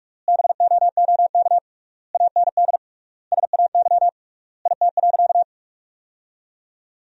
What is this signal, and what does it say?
Morse code, heard as a keyed tone, about 41 wpm, 700 Hz. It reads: BOOK AND SAY IT.
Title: 3x Word-Spacing - Sentences from Top 400 Words - 45wpm